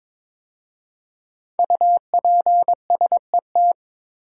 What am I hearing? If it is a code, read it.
Morse code, heard as a keyed tone, about 22 wpm, 700 Hz. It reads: UPSET
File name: 5 Letter Words - First 3 Letters Given - 22wpm